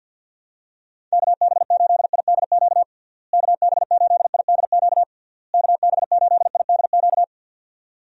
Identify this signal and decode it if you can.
Morse code, heard as a keyed tone, about 48 wpm, 700 Hz. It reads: KB8IDQ KB8IDQ KB8IDQ
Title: USA Call Signs - 6 Characters - Repeated 3x - 50wpm